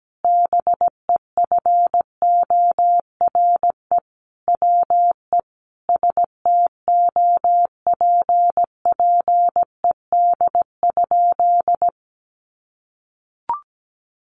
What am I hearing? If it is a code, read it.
Morse code, heard as a keyed tone, about 17 wpm, 700 Hz. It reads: BEFORE WE STOPPED?
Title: A Sound Of Thunder 17wpm